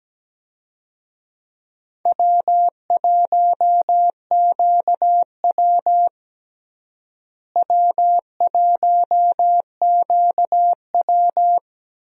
Morse code, heard as a keyed tone, about 17 wpm, 700 Hz. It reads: W1QW W1QW
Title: USA Call Signs - 4 Characters - Repeated 2x - 17wpm